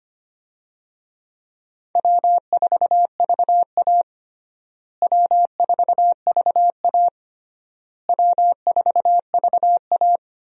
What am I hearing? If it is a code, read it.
Morse code, heard as a keyed tone, about 25 wpm, 700 Hz. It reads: W4VA W4VA W4VA
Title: USA Call Signs - 4 Characters - Repeated 3x - 25wpm